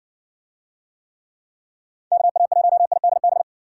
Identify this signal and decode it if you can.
Morse code, heard as a keyed tone, about 60 wpm, 700 Hz. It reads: KA1IDB